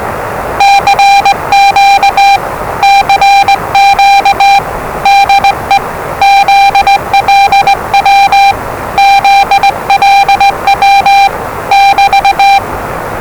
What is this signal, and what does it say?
Morse code, heard as a keyed tone, about 20 wpm, 790 Hz, with noise. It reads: CQ CQ DE ZLW ZLW <BT>
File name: WellingtonRadioZlwCloseDownOnHfAndMf / 01-Zlw 4277khz 417.5 qrt until 0925z